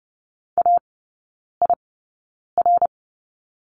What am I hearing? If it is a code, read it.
Morse code, heard as a keyed tone, about 30 wpm, spaced as for 9 wpm, 700 Hz. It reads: AIR
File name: course / 30:10wpm / 087 Lesson 15e - Number 5 - Three Characters 30wpm